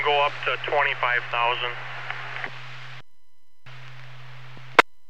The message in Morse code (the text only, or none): none